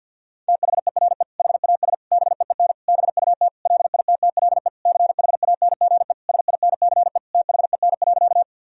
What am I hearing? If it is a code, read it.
Morse code, heard as a keyed tone, about 52 wpm, 700 Hz. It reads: THERE HAS BEEN BUT LITTLE CHANGE SINCE THEN.
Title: Sentences from Top 1000 Words - Review Part 1 - 50wpm